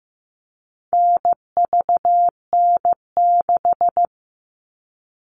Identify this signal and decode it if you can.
Morse code, heard as a keyed tone, about 15 wpm, 700 Hz. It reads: NVN6
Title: All Cut Numbers - 4-Digit 15wpm